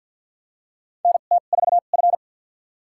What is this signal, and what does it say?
Morse code, heard as a keyed tone, about 49 wpm, spaced as for 36 wpm, 700 Hz. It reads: NT4F
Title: USA Call Signs - 4 Characters - 2x Character Spacing - 50wpm